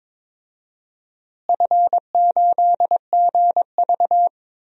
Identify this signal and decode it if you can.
Morse code, heard as a keyed tone, about 22 wpm, 700 Hz. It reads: F8GV
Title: DXCC-100 Call Signs - 4 Characters - Repeated 2x - 22wpm